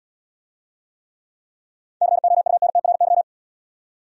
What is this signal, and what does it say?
Morse code, heard as a keyed tone, about 65 wpm, 700 Hz. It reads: COUNTRY